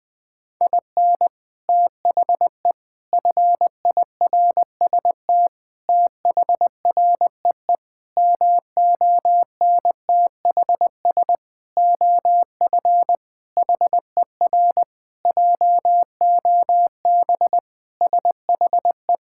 Morse code, heard as a keyed tone, about 21 wpm, 700 Hz. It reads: IN THE FIRST THREE MONTHS OF HER JOB SHE
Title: Sentences with 6-Word Phrases - 20wpm